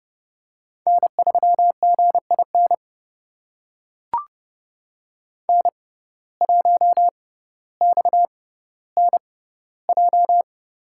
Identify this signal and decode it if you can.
Morse code, heard as a keyed tone, about 30 wpm, spaced as for 15 wpm, 700 Hz. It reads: N3GIN N1XNJ
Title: USA Call Signs - 5 Characters - 6x Character Spacing - 30wpm